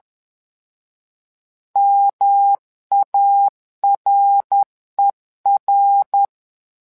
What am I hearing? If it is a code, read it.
Morse code, heard as a keyed tone, about 10 wpm, 800 Hz. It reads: MARER
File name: LEZIONI / 3 MWRO / 3 MWRO 10wpm 2148